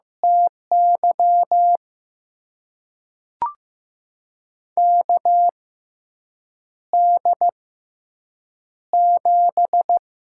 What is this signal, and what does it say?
Morse code, heard as a keyed tone, about 15 wpm, spaced as for 7 wpm, 700 Hz. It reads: TY KD7
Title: USA Call Signs - 6x Character Spacing - 15wpm